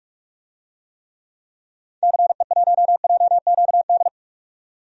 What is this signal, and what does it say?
Morse code, heard as a keyed tone, about 46 wpm, 700 Hz. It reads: CE1JQD
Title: DXCC-100 Call Signs - 6 Characters - 45wpm